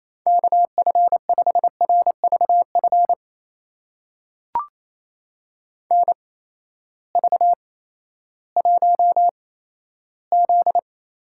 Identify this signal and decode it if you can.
Morse code, heard as a keyed tone, about 28 wpm, spaced as for 14 wpm, 700 Hz. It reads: KF5RVF NV1Z
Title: USA Call Signs - 8x Character Spacing - 28wpm